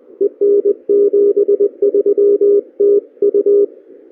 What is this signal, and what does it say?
Morse code, heard as a keyed tone, about 21 wpm, 410 Hz, with radio noise. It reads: EN73TU
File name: CY0X 6m CW 1232Z